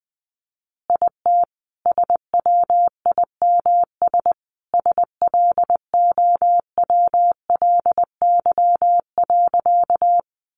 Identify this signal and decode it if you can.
Morse code, heard as a keyed tone, about 20 wpm, 700 Hz. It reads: IT SWIMS SLOWLY.
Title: Easy Phrases 20wpm